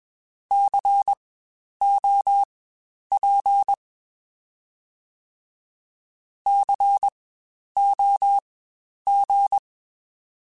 Morse code, heard as a keyed tone, about 21 wpm, spaced as for 10 wpm, 800 Hz. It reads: COP COG
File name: Spaced Alphas / Spaced Alphas mp3s / Cs sp @20